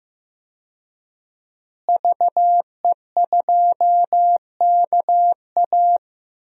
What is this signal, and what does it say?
Morse code, heard as a keyed tone, about 15 wpm, 700 Hz. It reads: VE2KA
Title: DXCC-100 Call Signs - 5 Characters - Repeated 2x - 15wpm